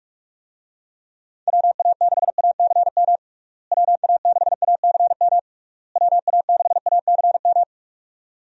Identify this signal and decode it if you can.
Morse code, heard as a keyed tone, about 41 wpm, 700 Hz. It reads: WA6ACM WA6ACM WA6ACM
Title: USA Call Signs - 6 Characters - Repeated 3x - 45wpm